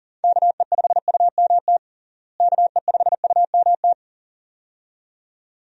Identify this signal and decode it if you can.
Morse code, heard as a keyed tone, about 40 wpm, 700 Hz. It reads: KE5UMT KE5UMT
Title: USA Call Signs - 6 Characters - Repeated 2x - 40wpm